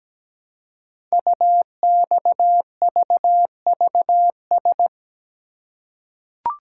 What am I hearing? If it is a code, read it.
Morse code, heard as a keyed tone, about 17 wpm, 700 Hz. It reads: UXVVS
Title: Code Groups - 5 Letters - 17wpm